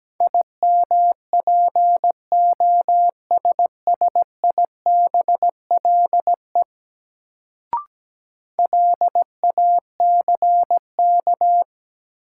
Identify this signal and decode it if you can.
Morse code, heard as a keyed tone, about 17 wpm, 700 Hz. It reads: IMPOSSIBLE LACK
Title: Sentences from Top 1100 Words - New Word Drill - 17wpm